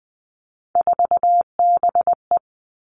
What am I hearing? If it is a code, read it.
Morse code, heard as a keyed tone, about 20 wpm, 700 Hz. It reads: 4BE
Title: Three Challenging Letter-Number Combinations 20wpm